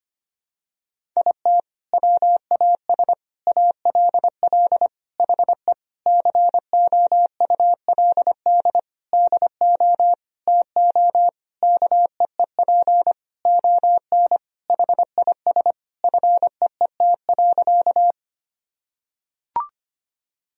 Morse code, heard as a keyed tone, about 25 wpm, 700 Hz. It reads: IT WAS ALL HE COULD DO TO KEEP ON HIS FEET.
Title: Sentences from Top 1000 Words - Review Part 1 - 25wpm